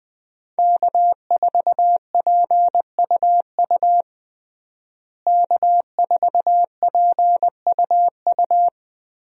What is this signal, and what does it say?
Morse code, heard as a keyed tone, about 20 wpm, 700 Hz. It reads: K4PUU K4PUU
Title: USA Call Signs - 5 Characters - Repeated 3x - 20wpm